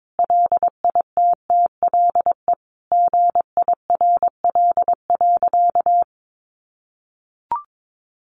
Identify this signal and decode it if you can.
Morse code, heard as a keyed tone, about 22 wpm, 700 Hz. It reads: LITTLE GIRL.
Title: Sentences from Top 1200 Words 22wpm